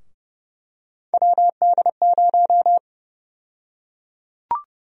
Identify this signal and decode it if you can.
Morse code, heard as a keyed tone, about 30 wpm, 700 Hz. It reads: WD0
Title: USA Call Sign Prefixes - Repeated 2x - 30wpm